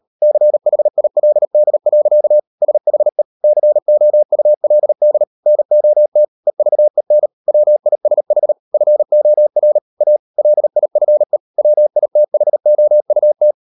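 Morse code, heard as a keyed tone, about 38 wpm, 600 Hz. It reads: CHILD. SHE COULD NOT EVEN WISH FOR A LIFE WITHOUT
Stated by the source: Morse code practice at 38 WPM, sending Ken Follett's novel Never in English, from CW med Gnister.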